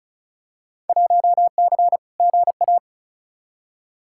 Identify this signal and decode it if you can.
Morse code, heard as a keyed tone, about 36 wpm, 700 Hz. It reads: 1C GA
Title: Field Day Exchanges - 35wpm